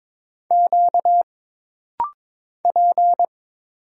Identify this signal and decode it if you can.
Morse code, heard as a keyed tone, about 22 wpm, 700 Hz. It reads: Q P
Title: Single Letter-Number Alt - Rapid-Fire 22wpm